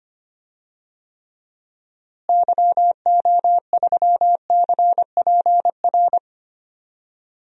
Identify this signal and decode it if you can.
Morse code, heard as a keyed tone, about 25 wpm, 700 Hz. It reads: YO3CPR
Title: DXCC-100 Call Signs - 6 Characters - Repeated 2x - 25wpm